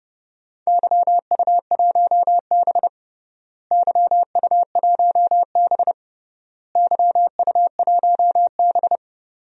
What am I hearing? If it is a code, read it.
Morse code, heard as a keyed tone, about 32 wpm, 700 Hz. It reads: YU1B YU1B YU1B
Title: DXCC-100 Call Signs - 4 Characters - Repeated 3x - 30wpm